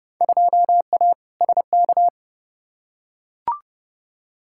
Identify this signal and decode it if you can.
Morse code, heard as a keyed tone, about 30 wpm, 700 Hz. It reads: 2A SK
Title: Field Day Exchanges Encore - 30wpm